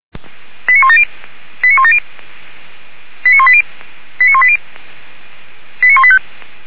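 Five-tone selective call (ZVEI), heard as quick runs of five tones, tones of about 70 ms. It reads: 89170 89170 89180 89180 89185
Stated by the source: ZVEI five-tone selective call received by SDR.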